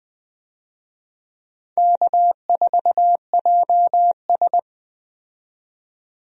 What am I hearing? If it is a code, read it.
Morse code, heard as a keyed tone, about 20 wpm, 700 Hz. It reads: K4JS
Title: USA Call Signs - 4 Characters - 20wpm